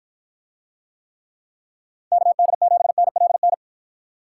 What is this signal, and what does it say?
Morse code, heard as a keyed tone, about 53 wpm, 700 Hz. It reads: KD7NLN